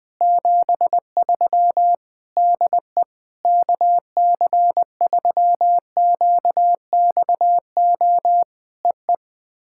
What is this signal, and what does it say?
Morse code, heard as a keyed tone, about 18 wpm, 700 Hz. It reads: 73 DE KC3QXO EE
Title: POTA QSO Practice - 20wpm